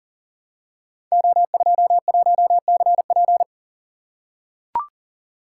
Morse code, heard as a keyed tone, about 40 wpm, 700 Hz. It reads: O21CP
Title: Code Groups - 5 Letter:Numbers - 40wpm